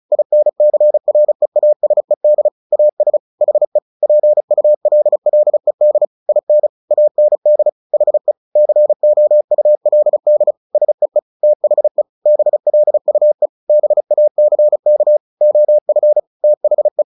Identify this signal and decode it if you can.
Morse code, heard as a keyed tone, about 34 wpm, 600 Hz. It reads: INCREASED AS HE PULLED IN AND HE COULD SEE THE BLUE BACK OF THE
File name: the old man and the sea / the old man and the sea 1 35wpm